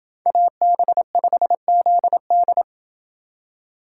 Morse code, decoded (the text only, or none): AB5ZD